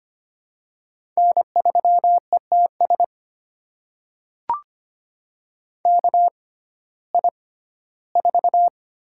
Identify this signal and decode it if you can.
Morse code, heard as a keyed tone, about 25 wpm, spaced as for 15 wpm, 700 Hz. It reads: N3ETS KI4
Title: USA Call Signs - 6x Character Spacing - 25wpm